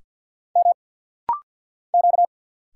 Morse code, heard as a keyed tone, about 49 wpm, 700 Hz. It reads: M Q
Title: Single Letter-Number Alt - Rapid-Fire 50wpm